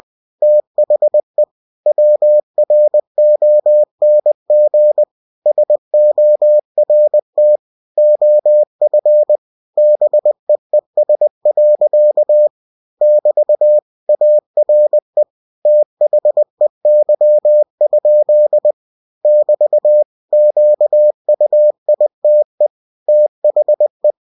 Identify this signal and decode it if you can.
Morse code, heard as a keyed tone, about 20 wpm, 600 Hz. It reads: THE WRONG SORT OF BEES. <BT> ARE THEY? <BT> QUITE THE
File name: cw-books / winnie-the-pooh / winnie-20wpm-page-0009